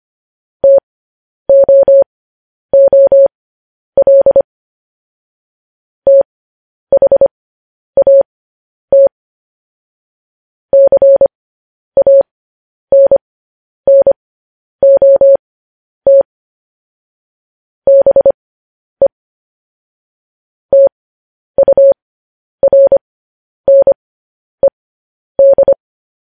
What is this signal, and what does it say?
Morse code, heard as a keyed tone, about 25 wpm, spaced as for 10 wpm, 550 Hz. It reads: TOOL THAT CANNOT BE TURNED
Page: news headlines in morse code at 10 wpm